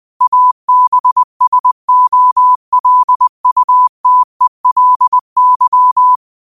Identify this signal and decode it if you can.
Morse code, heard as a keyed tone, about 20 wpm, 1000 Hz. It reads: ABSOLUTELY
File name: qotdfiles / 20wpm